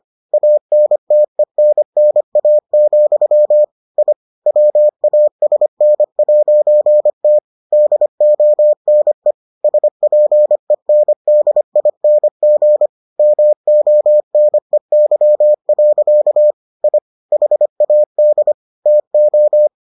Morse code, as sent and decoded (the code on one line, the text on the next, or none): .- -. - . -. -. .- --..-- / .. / .-- .- ... -. .----. - / -.. --- -. / ... .--. -. -.. .. -. --. / -- --- -. -.-- .-.-.- / .. / .... .- -.. / - ---
ANTENNA, I WASN'T DONE SPENDING MONEY. I HAD TO